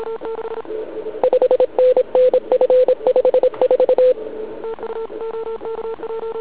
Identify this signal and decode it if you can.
Morse code, heard as a keyed tone, about 27 wpm, 510 Hz. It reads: 5NN F54
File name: ARCHIV / zvuk2 / CW RUN